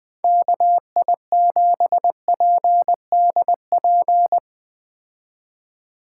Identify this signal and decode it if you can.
Morse code, heard as a keyed tone, about 20 wpm, 700 Hz. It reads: KI7PDP
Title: Speed Racing - USA 2x3 Call Signs 17wpm